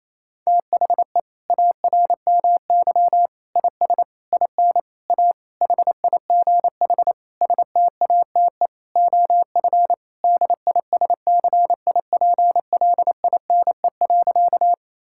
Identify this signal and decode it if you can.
Morse code, heard as a keyed tone, about 28 wpm, 700 Hz. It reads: THE ARMY IS IN A HIGH STATE OF DISCIPLINE.
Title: Sentences from Top 1000 Words - Review Part 6 - 28wpm